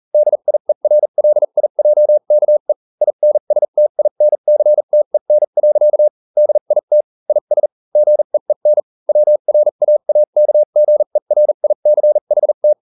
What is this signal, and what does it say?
Morse code, heard as a keyed tone, about 40 wpm, spaced as for 35 wpm, 600 Hz. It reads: DIERLIJKE INSTINCTEN. DIT IS GEEN WRAAKGERICHT